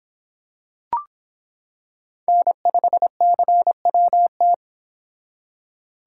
Morse code, heard as a keyed tone, about 25 wpm, 700 Hz. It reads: N5CWT